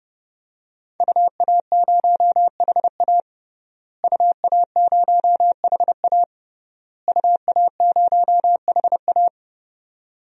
Morse code, heard as a keyed tone, about 30 wpm, 700 Hz. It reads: UA0HA UA0HA UA0HA
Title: DXCC-100 Call Signs - 5 Characters - Repeated 3x - 30wpm